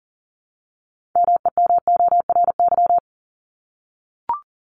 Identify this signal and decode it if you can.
Morse code, heard as a keyed tone, about 40 wpm, 700 Hz. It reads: MEMORY
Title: Top 1251-1500 Words 40wpm